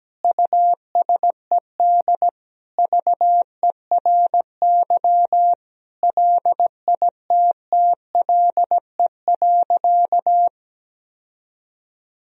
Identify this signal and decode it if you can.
Morse code, heard as a keyed tone, about 17 wpm, 700 Hz. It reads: USED VERY LITTLE.
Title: Sentences from Top 1000 Words - Review Part 3 - 17wpm